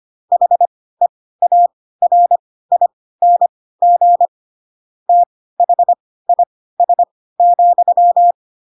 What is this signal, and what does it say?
Morse code, heard as a keyed tone, about 25 wpm, spaced as for 16 wpm, 700 Hz. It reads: HEARING THIS,
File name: Pinocchio / Pinocchio ch3 10 25 ik1hge